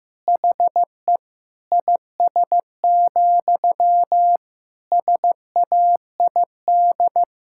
Morse code, heard as a keyed tone, about 15 wpm, 700 Hz. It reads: HE IS, SAID
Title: Sentences from Top 1600 Words 15wpm